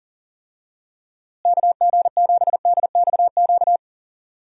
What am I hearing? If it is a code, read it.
Morse code, heard as a keyed tone, about 40 wpm, 700 Hz. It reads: KG7DXQ
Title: USA Call Signs - 6 Characters - Repeated 2x - 40wpm